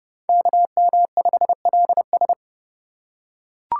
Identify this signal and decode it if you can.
Morse code, heard as a keyed tone, about 30 wpm, 700 Hz. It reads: KM5LS